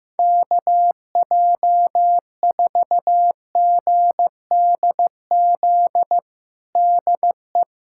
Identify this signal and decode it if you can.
Morse code, heard as a keyed tone, about 15 wpm, 700 Hz. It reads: KJ4GDZ DE
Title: USA Callsign DE Callsign 15wpm